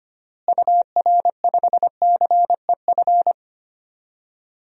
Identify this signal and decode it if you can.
Morse code, heard as a keyed tone, about 25 wpm, 700 Hz. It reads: UR5CEF
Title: DXCC-100 Call Signs - 6 Characters - Repeated 3x - 25wpm